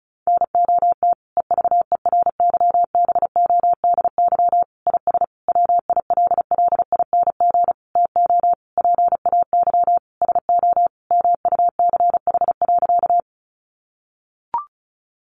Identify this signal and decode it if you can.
Morse code, heard as a keyed tone, about 34 wpm, 700 Hz. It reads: NOT EVERYBODY IS WILLING TO PAY SO MUCH.
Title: Sentences from Top 1400 Words 35wpm